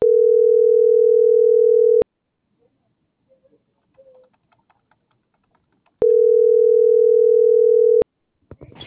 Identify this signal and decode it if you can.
Ringback tone (North America), 440 + 480 Hz.